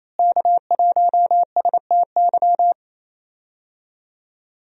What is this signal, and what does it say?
Morse code, heard as a keyed tone, about 28 wpm, 700 Hz. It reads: K1STY
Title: USA Call Signs - 5 Characters - Repeated 3x - 28wpm